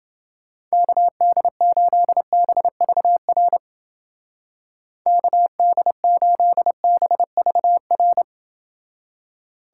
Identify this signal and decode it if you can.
Morse code, heard as a keyed tone, about 31 wpm, 700 Hz. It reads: KD8BVR KD8BVR